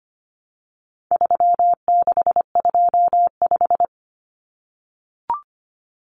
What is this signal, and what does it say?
Morse code, heard as a keyed tone, about 25 wpm, 700 Hz. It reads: 3625
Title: Common Cut Numbers - 4-Digit 25wpm